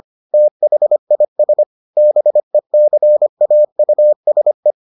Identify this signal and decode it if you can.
Morse code, heard as a keyed tone, about 25 wpm, 600 Hz. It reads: THIS BECAUSE